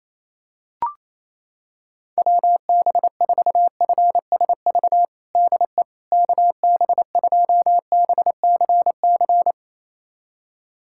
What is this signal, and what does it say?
Morse code, heard as a keyed tone, about 28 wpm, 700 Hz. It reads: WB4FSV DE KB2BCC
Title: USA Callsign DE Callsign - 6 Characters - 28wpm